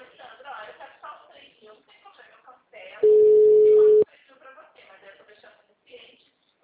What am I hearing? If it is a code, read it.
Ringback tone (Europe, 425 Hz), with speech.
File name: out- -2009- - - .61